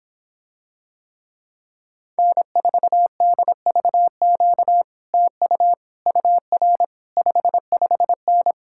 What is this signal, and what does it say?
Morse code, heard as a keyed tone, about 26 wpm, 700 Hz. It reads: N4DVQ TU UR 55N